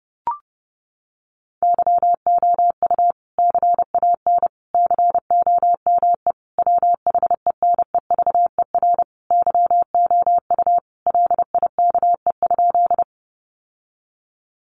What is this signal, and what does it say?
Morse code, heard as a keyed tone, about 30 wpm, 700 Hz. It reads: YOU CAN COME WHENEVER YOU LIKE?
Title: Sentences from Top 1200 Words 30wpm